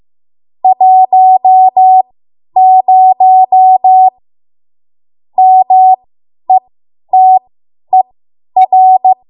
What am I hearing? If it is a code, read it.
Morse code, heard as a keyed tone, about 15 wpm, spaced as for 10 wpm, 750 Hz. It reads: 10 METER
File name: Morse / 10WPM